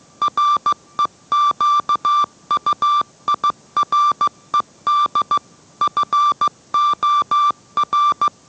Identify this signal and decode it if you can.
Morse code, heard as a keyed tone, about 15 wpm, 1180 Hz. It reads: REQUIRED FOR